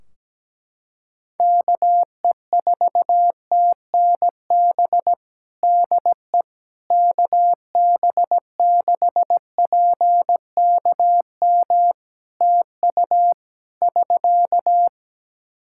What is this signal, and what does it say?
Morse code, heard as a keyed tone, about 17 wpm, 700 Hz. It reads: KE4TNB DE KB6PKM TU <SK>
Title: Practice QSO - Exchange 4 - Ending - 17wpm